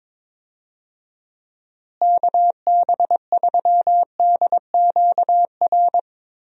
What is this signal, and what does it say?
Morse code, heard as a keyed tone, about 22 wpm, 700 Hz. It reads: KB3DQR
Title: USA Call Signs - 6 Characters - Repeated 3x - 22wpm